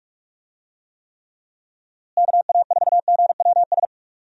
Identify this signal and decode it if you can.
Morse code, heard as a keyed tone, about 45 wpm, 700 Hz. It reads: KA4GWS
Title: USA Call Signs - 6 Characters - Repeated 2x - 45wpm